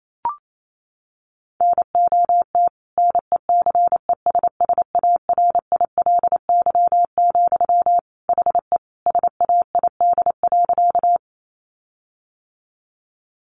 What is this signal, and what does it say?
Morse code, heard as a keyed tone, about 28 wpm, 700 Hz. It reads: NOT NECESSARILY, HE SAID.
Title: Sentences from Top 1400 Words 28wpm